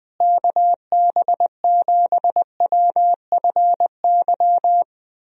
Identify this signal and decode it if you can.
Morse code, heard as a keyed tone, about 20 wpm, 700 Hz. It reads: KB7WFY
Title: USA Call Signs - 6 Characters - Repeated 2x - 20wpm